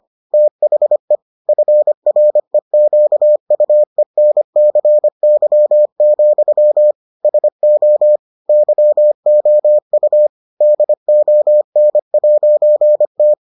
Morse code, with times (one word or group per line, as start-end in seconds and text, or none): 0.3-1.2 THE
1.5-6.9 FREQUENCY,
7.2-8.2 SO
8.5-10.3 YOU
10.6-13.3 DON'T